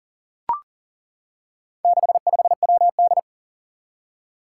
Morse code, heard as a keyed tone, about 42 wpm, 700 Hz. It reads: 65WD
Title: Four Challenging Letter-Number Combinations 40wpm